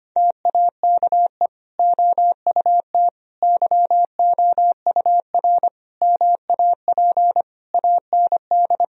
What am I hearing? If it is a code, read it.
Morse code, heard as a keyed tone, about 25 wpm, 700 Hz. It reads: TAKE OUT YOUR MAP AND
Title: Sentences from Top 1000 Words - Review Part 2 - 25wpm